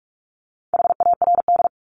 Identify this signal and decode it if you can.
Morse code, heard as a keyed tone, about 44 wpm, 700 Hz. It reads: HARD
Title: Top 500 Words 45wpm